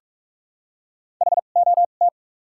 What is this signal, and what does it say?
Morse code, heard as a keyed tone, about 45 wpm, spaced as for 32 wpm, 700 Hz. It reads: HOT
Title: 2x Character Spacing - Top 500 Words - 45wpm